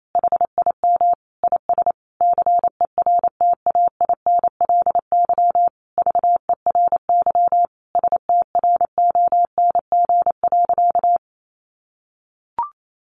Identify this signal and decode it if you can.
Morse code, heard as a keyed tone, about 28 wpm, 700 Hz. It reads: HIM IS CERTAINLY VERY STRONG.
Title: Sentences from Top 1300 Words 28wpm